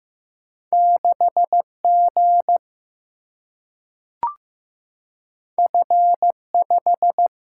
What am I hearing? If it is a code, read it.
Morse code, heard as a keyed tone, about 15 wpm, 700 Hz. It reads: 6 G F 5
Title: Single-Letter Single-Number 15wpm